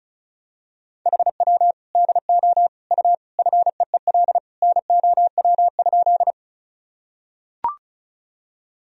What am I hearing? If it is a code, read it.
Morse code, heard as a keyed tone, about 36 wpm, 700 Hz. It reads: HW DO U FEEL NOW?
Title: Sentences from Top 400 Words and Abbreviations 35wpm